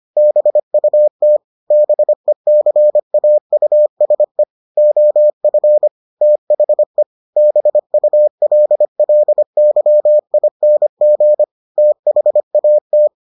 Morse code, heard as a keyed tone, about 25 wpm, 600 Hz. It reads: BUT BECAUSE OF THE BULLYING THAT